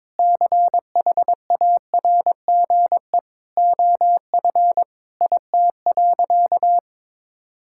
Morse code, heard as a keyed tone, about 22 wpm, 700 Hz. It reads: CHARGE OF IT.
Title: Sentences from Top 1000 Words - Review Part 6 - 22wpm